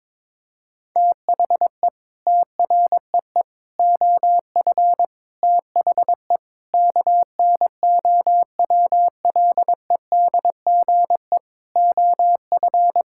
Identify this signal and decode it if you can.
Morse code, heard as a keyed tone, about 22 wpm, 700 Hz. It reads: THE TREE OF THE KNOWLEDGE OF
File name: Sets of 9 Words - 22wpm